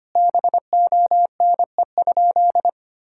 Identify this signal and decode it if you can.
Morse code, heard as a keyed tone, about 25 wpm, 700 Hz. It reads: BONE?
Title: Sentences from Top 1600 Words 25wpm